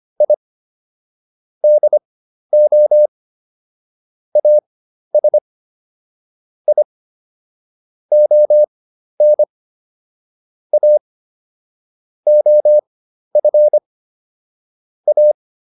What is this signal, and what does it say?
Morse code, heard as a keyed tone, about 25 wpm, spaced as for 12 wpm, 600 Hz. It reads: I DO AS I ON A OF A